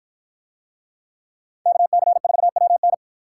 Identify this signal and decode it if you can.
Morse code, heard as a keyed tone, about 53 wpm, 700 Hz. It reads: KC4WN